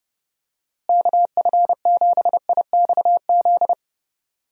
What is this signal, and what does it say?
Morse code, heard as a keyed tone, about 32 wpm, 700 Hz. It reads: KF7IXZ
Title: USA Call Signs - 6 Characters - Repeated 2x - 30wpm